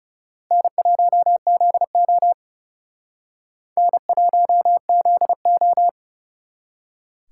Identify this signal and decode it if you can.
Morse code, heard as a keyed tone, about 39 wpm, 700 Hz. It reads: N1ZO N1ZO